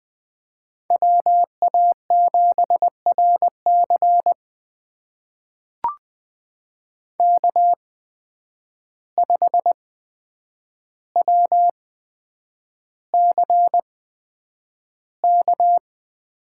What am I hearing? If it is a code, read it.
Morse code, heard as a keyed tone, about 20 wpm, spaced as for 9 wpm, 700 Hz. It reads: WA7RC K5WCK
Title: USA Call Signs - 5 Characters - 8x Character Spacing - 20wpm